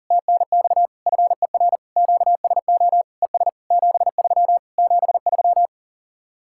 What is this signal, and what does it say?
Morse code, heard as a keyed tone, about 50 wpm, 700 Hz. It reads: TNX FER QSO ES 73 73
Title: Speed-Racing Common QSO Phrases Encore - 40wpm